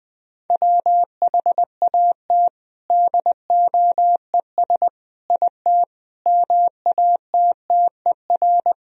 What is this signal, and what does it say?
Morse code, heard as a keyed tone, about 20 wpm, 700 Hz. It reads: WHAT DOES IT MATTER
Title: Sentences from Top 1000 Words - Review Part 3 - 20wpm